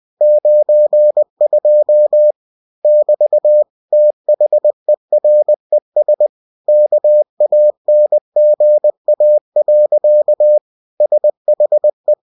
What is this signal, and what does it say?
Morse code, heard as a keyed tone, about 20 wpm, 600 Hz. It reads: 92 <BT> THERES KANGA. SHE
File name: cw-books / winnie-the-pooh / winnie-20wpm-page-0092